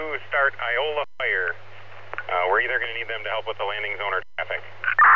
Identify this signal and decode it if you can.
Emergency Alert System (SAME) tones, heard as harsh buzzing bursts, with speech. It is heard from 1.2 to 4.9 s.